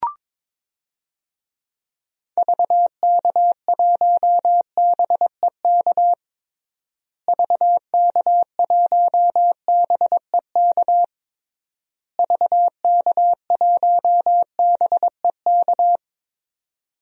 Morse code, heard as a keyed tone, about 22 wpm, 700 Hz. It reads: VK1BEK VK1BEK VK1BEK
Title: DXCC-100 Call Signs - 6 Characters - Repeated 3x - 22wpm